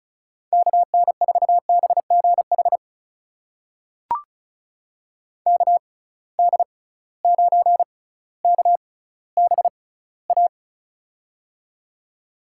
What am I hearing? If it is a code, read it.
Morse code, heard as a keyed tone, about 36 wpm, spaced as for 18 wpm, 700 Hz. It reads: KN4BGH KD9KBA
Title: USA Call Signs - 6 Characters - 6x Character Spacing - 35wpm